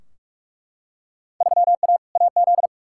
Morse code, heard as a keyed tone, about 43 wpm, 700 Hz. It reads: 3A AZ
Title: Field Day Exchanges - 45wpm